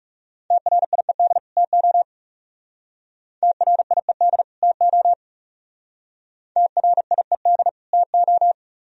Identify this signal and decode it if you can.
Morse code, heard as a keyed tone, about 37 wpm, 700 Hz. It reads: TRIED TO TRIED TO TRIED TO
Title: Speed-Racing - Sets of 2 Words 30wpm